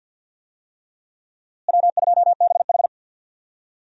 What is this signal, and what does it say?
Morse code, heard as a keyed tone, about 47 wpm, 700 Hz. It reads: W2BH
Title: USA Call Signs - 4 Characters - Repeated 3x - 50wpm